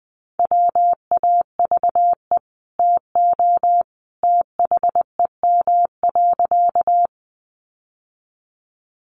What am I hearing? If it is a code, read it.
Morse code, heard as a keyed tone, about 20 wpm, 700 Hz. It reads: WAVE TO THEM.
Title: Sentences from Top 1400 Words 20wpm